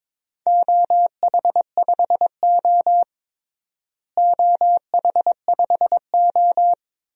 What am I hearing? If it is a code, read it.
Morse code, heard as a keyed tone, about 22 wpm, 700 Hz. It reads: OH5O OH5O
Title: DXCC-100 Call Signs - 4 Characters - Repeated 2x - 22wpm